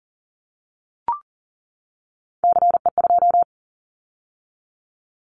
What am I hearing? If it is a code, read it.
Morse code, heard as a keyed tone, about 40 wpm, 700 Hz. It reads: CE2
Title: Callsign Prefixes for Top 100 DXCC - 40wpm